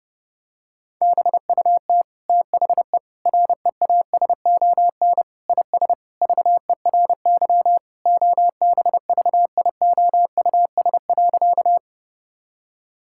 Morse code, heard as a keyed tone, about 30 wpm, 700 Hz. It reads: BUT THE REASON IS VERY OBVIOUS.
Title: Sentences from Top 1600 Words 30wpm